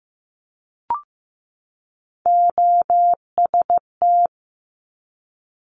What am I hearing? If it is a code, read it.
Morse code, heard as a keyed tone, about 15 wpm, 700 Hz. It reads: OST
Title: Multi-letter Phonemes 15wpm